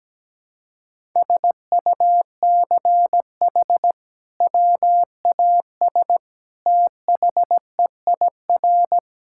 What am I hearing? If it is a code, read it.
Morse code, heard as a keyed tone, about 17 wpm, 700 Hz. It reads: SUCH WAS THEIR
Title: Sentences from Top 1000 Words - Review Part 5 - 17wpm